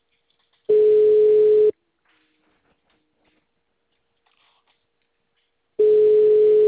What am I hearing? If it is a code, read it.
Ringback tone (Europe, 425 Hz).
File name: out- -2038- - - .39005